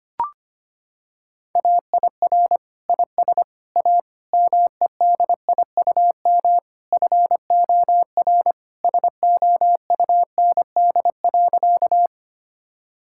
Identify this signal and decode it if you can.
Morse code, heard as a keyed tone, about 25 wpm, 700 Hz. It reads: AIR IS A MEDIUM FOR SOUND.
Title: Sentences from Top 1000 Words - Review Part 6 - 25wpm